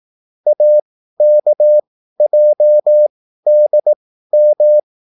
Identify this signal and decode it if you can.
Morse code, heard as a keyed tone, about 18 wpm, spaced as for 13 wpm, 600 Hz. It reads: AKJDM